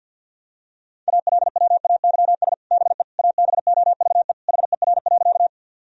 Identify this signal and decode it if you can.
Morse code, heard as a keyed tone, about 52 wpm, 700 Hz. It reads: ALWAYS BE ABOVE HER.
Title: Sentences from Top 1000 Words - Review Part 2 - 50wpm